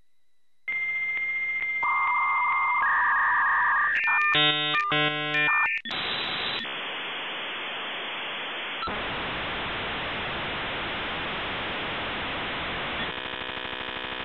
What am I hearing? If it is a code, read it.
Dial-up modem handshake.
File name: B8-Old Modem Sounds (300 baud to 56K)